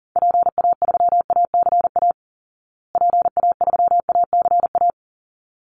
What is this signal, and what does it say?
Morse code, heard as a keyed tone, about 40 wpm, 700 Hz. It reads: PA3ACA PA3ACA
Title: Call Signs Repeated 3x - 40wpm